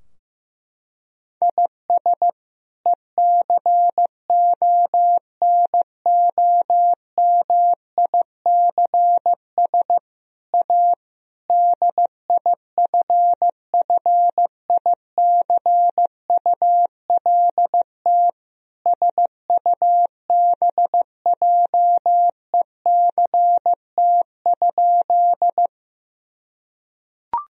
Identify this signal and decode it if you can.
Morse code, heard as a keyed tone, about 16 wpm, 700 Hz. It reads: IS ECONOMICS A DIFFICULT SUBJECT?
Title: Sentences from Top 1000 Words - Review Part 3 - 15wpm